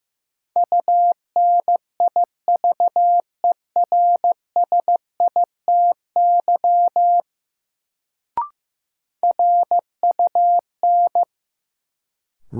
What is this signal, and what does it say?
Morse code, heard as a keyed tone, about 15 wpm, 700 Hz. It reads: UNIVERSITY RUN
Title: Sets of 4 Words - New Word Drill - 15wpm